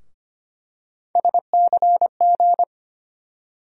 Morse code, heard as a keyed tone, about 25 wpm, 700 Hz. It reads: SCG
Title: USA Call Sign Postfixes - 3 Letters 25wpm